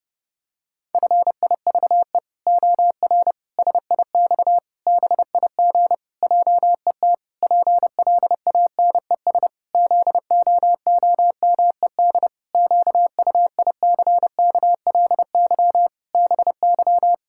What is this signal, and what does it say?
Morse code, heard as a keyed tone, about 32 wpm, 700 Hz. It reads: FIVE OR SIX BIG JET PLANES ZOOMED QUICKLY BY
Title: Pangram Challenge v2 - 30wpm